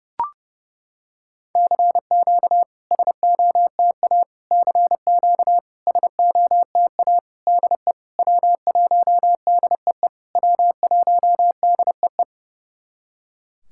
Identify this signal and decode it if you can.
Morse code, heard as a keyed tone, about 30 wpm, 700 Hz. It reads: CQ SOTA CQ SOTA DE W1DEE W1DEE
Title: SOTA QSO Practice 30wpm